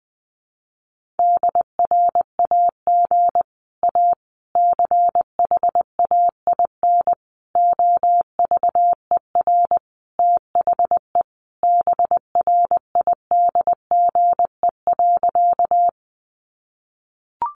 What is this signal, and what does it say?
Morse code, heard as a keyed tone, about 20 wpm, 700 Hz. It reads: DRAG A CHAIN OVER THE BRIDGE.
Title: Sentences from Top 1500 Words 20wpm